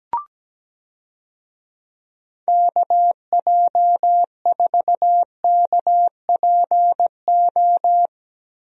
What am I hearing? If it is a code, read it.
Morse code, heard as a keyed tone, about 17 wpm, 700 Hz. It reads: KJ4KPO
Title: USA Call Signs - 6 Characters - Repeated 2x - 17wpm